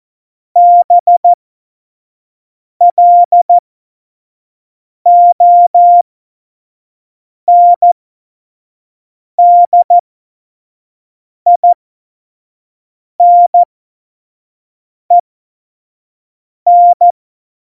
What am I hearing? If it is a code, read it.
Morse code, heard as a keyed tone, about 14 wpm, spaced as for 5 wpm, 700 Hz. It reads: BLONDINEN
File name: Morsekurs / Witze / Witz 11 Juli 17, Tempo 25 F70